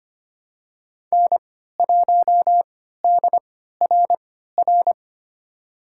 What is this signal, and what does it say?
Morse code, heard as a keyed tone, about 25 wpm, spaced as for 14 wpm, 700 Hz. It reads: N1DRR
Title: USA Call Signs - 3x Character Spacing - 25wpm